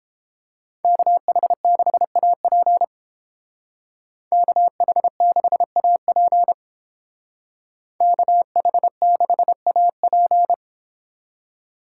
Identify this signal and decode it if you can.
Morse code, heard as a keyed tone, about 29 wpm, 700 Hz. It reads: KH6AP KH6AP KH6AP